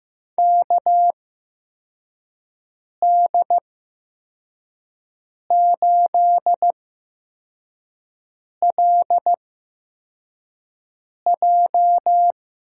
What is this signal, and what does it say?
Morse code, heard as a keyed tone, about 16 wpm, spaced as for 4 wpm, 700 Hz. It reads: KD8LJ
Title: USA Call Signs - 8x Character Spacing - 15wpm